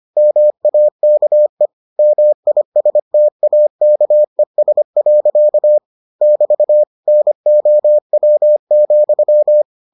Morse code, heard as a keyed tone, about 25 wpm, 600 Hz. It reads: MAKE MISTAKES. <BT> NOW,